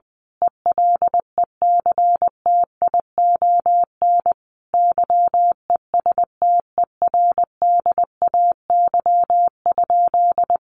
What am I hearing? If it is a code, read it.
Morse code, heard as a keyed tone, about 20 wpm, 700 Hz. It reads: ELECTION YESTERDAY?
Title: A Sound Of Thunder 20wpm